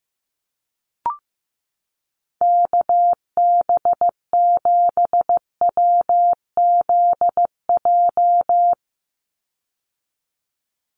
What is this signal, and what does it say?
Morse code, heard as a keyed tone, about 15 wpm, 700 Hz. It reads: KB7WZJ
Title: USA Call Signs 15wpm